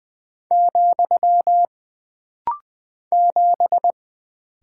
Morse code, heard as a keyed tone, about 20 wpm, 700 Hz. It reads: , 7
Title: Single Character (Letter:Number:Punctuation:Prosign) - Rapid-Fire - 20wpm